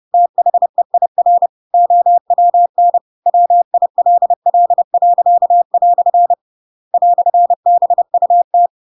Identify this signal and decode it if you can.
Morse code, heard as a keyed tone, about 30 wpm, 700 Hz. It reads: THEIR OWN WILL." "BUT
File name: Pinocchio / Pinocchio ch8 30 30 ik1hge